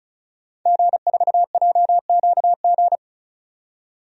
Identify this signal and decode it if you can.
Morse code, heard as a keyed tone, about 34 wpm, 700 Hz. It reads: G4JQG
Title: DXCC-100 Call Signs - 5 Characters - 35wpm